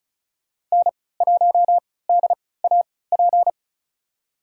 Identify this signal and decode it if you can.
Morse code, heard as a keyed tone, about 36 wpm, spaced as for 20 wpm, 700 Hz. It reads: N1DAP
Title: USA Call Signs - 5 Characters - 3x Character Spacing - 35wpm